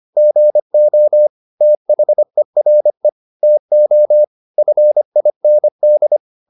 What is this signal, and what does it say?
Morse code, heard as a keyed tone, about 25 wpm, 600 Hz. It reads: GO THERE TO FIND
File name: the old man and the sea part 1 25wpm